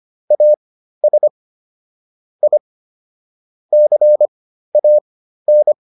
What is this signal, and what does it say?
Morse code, heard as a keyed tone, about 25 wpm, spaced as for 13 wpm, 600 Hz. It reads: AS I CAN
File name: WD404 13